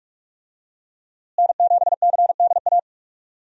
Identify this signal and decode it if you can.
Morse code, heard as a keyed tone, about 46 wpm, 700 Hz. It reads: N7CDA
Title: USA Call Signs - 5 Characters - Repeated 2x - 45wpm